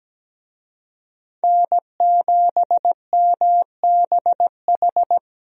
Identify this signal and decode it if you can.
Morse code, heard as a keyed tone, about 17 wpm, 700 Hz. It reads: N7MBH
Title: USA Call Signs - 5 Characters - 17wpm